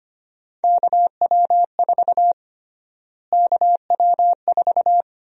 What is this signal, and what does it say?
Morse code, heard as a keyed tone, about 25 wpm, 700 Hz. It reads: KW4 KW4
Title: USA Call Sign Prefixes - Repeated 2x - 25wpm